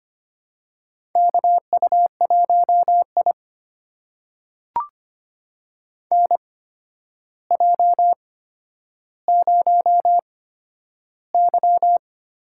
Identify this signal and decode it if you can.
Morse code, heard as a keyed tone, about 25 wpm, spaced as for 11 wpm, 700 Hz. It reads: KU1I NJ0Y
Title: USA Call Signs - 4 Characters - 8x Character Spacing - 25wpm